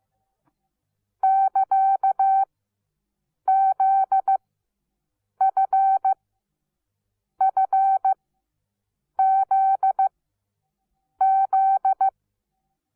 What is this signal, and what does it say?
Morse code, heard as a keyed tone, about 15 wpm, spaced as for 6 wpm, 770 Hz. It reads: <KA>ZFFZZ
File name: Onlinelehrgang / morsen / DJ4UF Morsen17